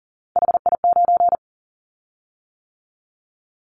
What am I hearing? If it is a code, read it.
Morse code, heard as a keyed tone, about 39 wpm, 700 Hz. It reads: HI9
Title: Callsign Prefixes for Top 100 DXCC - 40wpm